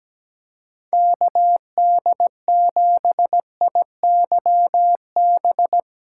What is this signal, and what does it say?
Morse code, heard as a keyed tone, about 17 wpm, 700 Hz. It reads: KD7IYB
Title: USA Call Signs - 6 Characters - 4x Character Spacing - 17wpm